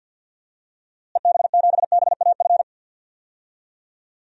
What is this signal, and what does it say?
Morse code, heard as a keyed tone, about 51 wpm, 700 Hz. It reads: EB7BAF
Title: DXCC-100 Call Signs - 6 Characters - Repeated 3x - 50wpm